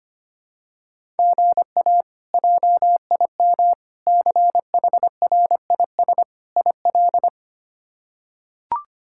Morse code, heard as a keyed tone, about 25 wpm, 700 Hz. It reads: GA JIM CHRIS IL
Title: K1USN SST QSO Practice - 25wpm